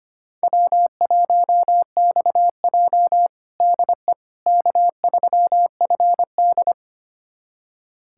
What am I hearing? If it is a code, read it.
Morse code, heard as a keyed tone, about 25 wpm, 700 Hz. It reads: W1XJ DE K3FD
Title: USA Callsign DE Callsign - 4 Characters - 25wpm